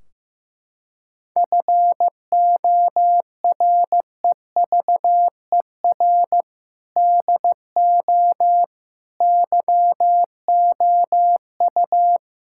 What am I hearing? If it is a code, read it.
Morse code, heard as a keyed tone, about 15 wpm, 700 Hz. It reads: FOREVER DO YOU
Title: Sentences from Top 1600 Words 15wpm